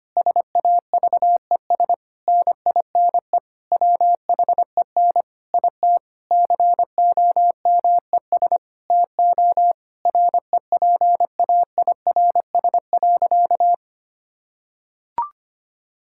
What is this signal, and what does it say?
Morse code, heard as a keyed tone, about 25 wpm, 700 Hz. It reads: SAVES NINE WHEN IT COMES TO REPAIRS.
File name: practice / 25wpm / Difficult Sentences with Hints - 25wpm